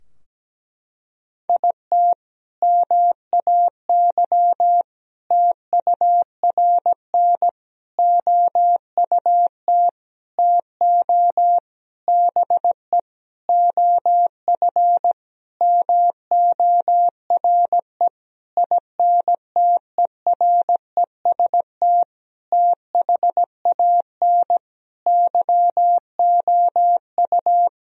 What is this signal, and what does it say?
Morse code, heard as a keyed tone, about 17 wpm, 700 Hz. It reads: IT MAY TURN OUT TO BE OF MORE INTEREST THAN YOU
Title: Sentences from Top 1000 Words - Review Part 2 - 17wpm